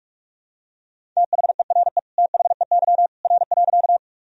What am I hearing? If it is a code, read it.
Morse code, heard as a keyed tone, about 46 wpm, 700 Hz. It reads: THERE THEY R.
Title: Sentences from Top 100 Words and Abbreviations 45wpm